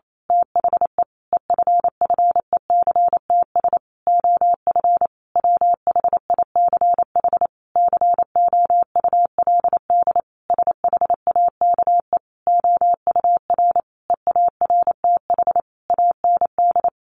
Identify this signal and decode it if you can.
Morse code, heard as a keyed tone, about 28 wpm, 700 Hz. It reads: THE EFFECTS OF WHICH COULD SHAKE OUR EARTH AND
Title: A Sound Of Thunder 28wpm